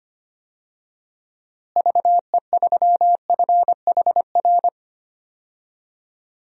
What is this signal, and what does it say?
Morse code, heard as a keyed tone, about 25 wpm, 700 Hz. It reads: VE3FHR
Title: DXCC-100 Call Signs - 6 Characters - Repeated 2x - 25wpm